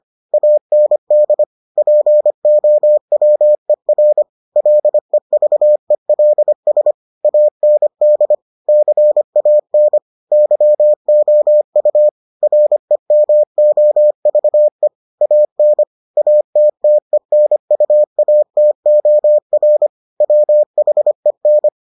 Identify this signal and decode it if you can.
Morse code, heard as a keyed tone, about 25 wpm, 600 Hz. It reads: AND POWER LEVELS AND CAN YOU REMOVE AN ATTENUATOR WHEN